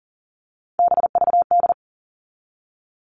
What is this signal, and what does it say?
Morse code, heard as a keyed tone, about 40 wpm, 700 Hz. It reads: BVD
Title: All Cut Numbers - 3-Digit 40wpm